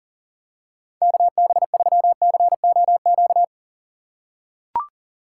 Morse code, heard as a keyed tone, about 40 wpm, 700 Hz. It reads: KB3COQ
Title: USA Call Signs - 6 Characters - 2x Character Spacing - 40wpm